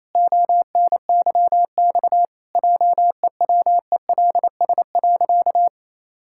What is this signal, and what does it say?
Morse code, heard as a keyed tone, about 28 wpm, 700 Hz. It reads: ONYX JEWELS.
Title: Pangram Challenge v2 - 28wpm